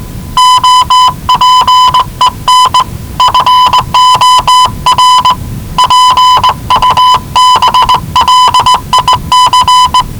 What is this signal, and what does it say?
Morse code, heard as a keyed tone, about 20 wpm, 1000 Hz. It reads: OPEN FOR PUBLIC
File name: WellingtonRadioZlwCloseDownOnHfAndMf / 01-Zlw 4277khz 417.5 qrt until 0925z